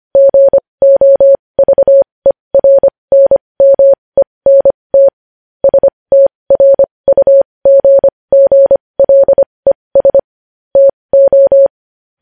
Morse code, heard as a keyed tone, about 25 wpm, spaced as for 20 wpm, 550 Hz. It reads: GOVERNMENT STRUGGLES TO
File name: podcast / News headlines-1x-25x20